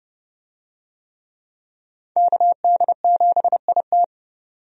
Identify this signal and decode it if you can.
Morse code, heard as a keyed tone, about 32 wpm, 700 Hz. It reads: KD7IT
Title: USA Call Signs - 5 Characters - 30wpm